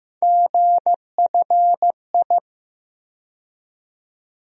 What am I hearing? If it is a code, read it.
Morse code, heard as a keyed tone, about 15 wpm, 700 Hz. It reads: GFI
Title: USA Call Sign Postfixes - 3 Letters 15wpm